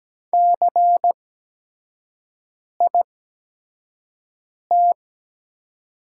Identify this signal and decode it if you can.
Morse code, heard as a keyed tone, about 17 wpm, spaced as for 5 wpm, 700 Hz. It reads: CIT
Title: 8x Character Spacing - Top 500 Words - 17wpm